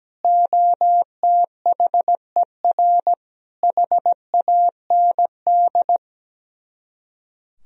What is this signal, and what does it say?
Morse code, heard as a keyed tone, about 17 wpm, 700 Hz. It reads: OTHER HAND
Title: Speed-Racing - Sets of 5 Words 17wpm